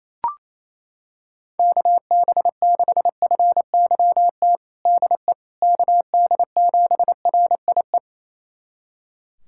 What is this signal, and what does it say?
Morse code, heard as a keyed tone, about 27 wpm, 700 Hz. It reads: KB6FYT DE KD7RIE
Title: USA Callsign DE Callsign - 6 Characters - 28wpm